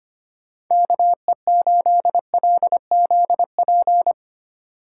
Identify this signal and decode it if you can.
Morse code, heard as a keyed tone, about 25 wpm, 700 Hz. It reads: KE8LZP
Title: USA Call Signs - 6 Characters - 8x Character Spacing - 25wpm